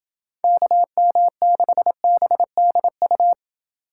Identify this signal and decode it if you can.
Morse code, heard as a keyed tone, about 27 wpm, 700 Hz. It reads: KM6BDU